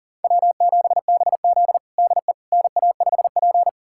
Morse code, heard as a keyed tone, about 40 wpm, 700 Hz. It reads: W7BZ DE NA5P
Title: USA Callsign DE Callsign - 4 Characters - 40wpm